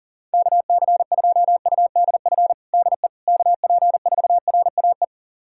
Morse code, heard as a keyed tone, about 37 wpm, 700 Hz. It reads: KC2UDF DE KP4RAE
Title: USA Callsign DE Callsign - 6 Characters - 40wpm